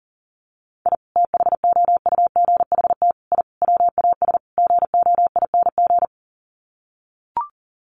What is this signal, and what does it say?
Morse code, heard as a keyed tone, about 40 wpm, 700 Hz. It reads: I THOUGHT I WAS GOING
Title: Sets of 5 Words Encore2 40wpm